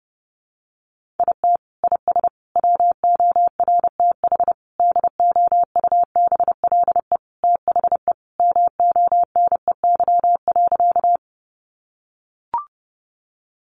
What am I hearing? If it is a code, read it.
Morse code, heard as a keyed tone, about 30 wpm, 700 Hz. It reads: IT IS WORTH DOUBLE THE MONEY.
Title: Sentences from Top 1400 Words 30wpm